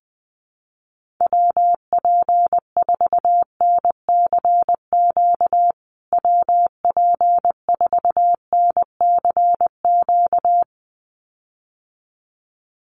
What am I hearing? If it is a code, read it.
Morse code, heard as a keyed tone, about 20 wpm, 700 Hz. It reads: WP4NCQ WP4NCQ
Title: USA Call Signs Repeated 2x 20wpm